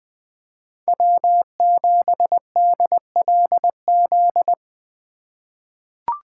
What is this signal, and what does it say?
Morse code, heard as a keyed tone, about 20 wpm, 700 Hz. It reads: W7DLZ
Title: 2023 POTA Activator Call Signs - 20wpm